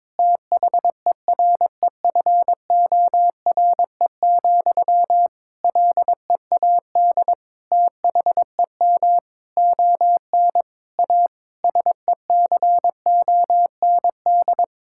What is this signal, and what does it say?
Morse code, heard as a keyed tone, about 22 wpm, 700 Hz. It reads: THEREFORE, LEAD THEM ON A SECOND
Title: Sentences from Top 1000 Words - Review Part 5 - 22wpm